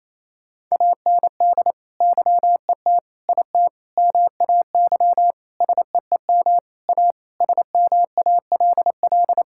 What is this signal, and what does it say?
Morse code, heard as a keyed tone, about 28 wpm, 700 Hz. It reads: AND YET IT MAY SEEM A SMALL
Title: Sentences from Top 1000 Words - Review Part 5 - 28wpm